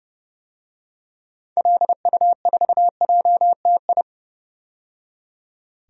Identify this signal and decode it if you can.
Morse code, heard as a keyed tone, about 30 wpm, 700 Hz. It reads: LU4JTI
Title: DXCC-100 Call Signs - 6 Characters - 30wpm